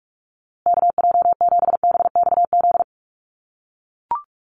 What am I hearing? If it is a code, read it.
Morse code, heard as a keyed tone, about 44 wpm, 700 Hz. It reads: KJ7BXZ
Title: USA Call Signs Repeated 3x 45wpm